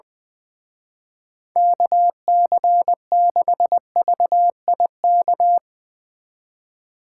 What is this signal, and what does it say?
Morse code, heard as a keyed tone, about 21 wpm, 700 Hz. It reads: KC6VIK
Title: Speed Racing - USA 2x3 Call Signs 20wpm